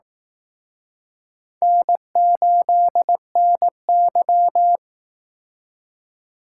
Morse code, heard as a keyed tone, about 19 wpm, 700 Hz. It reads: N8NY